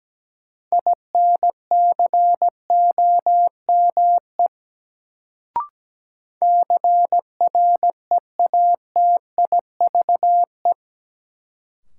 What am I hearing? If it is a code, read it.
Morse code, heard as a keyed tone, about 17 wpm, 700 Hz. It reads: INCOME CREATIVE
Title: Sentences from Top 1000 Words - Review Part 5 - New Word Drill - 17wpm